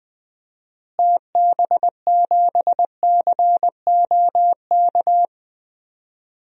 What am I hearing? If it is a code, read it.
Morse code, heard as a keyed tone, about 20 wpm, 700 Hz. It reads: TB7COK
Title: DXCC-100 Call Signs - 6 Characters - Repeated 3x - 20wpm